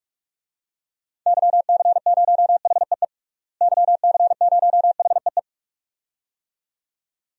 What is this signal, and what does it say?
Morse code, heard as a keyed tone, about 42 wpm, 700 Hz. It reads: YC0HEE YC0HEE
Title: DXCC-100 Call Signs - 6 Characters - Repeated 2x - 45wpm